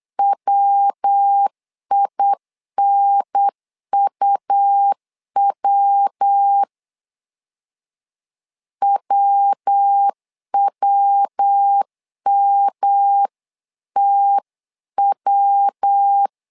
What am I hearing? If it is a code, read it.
Morse code, heard as a keyed tone, about 8 wpm, 800 Hz. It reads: WINUW WWMTW
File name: LEZIONI / 3 MWRO / 3 MW 8wpm 2501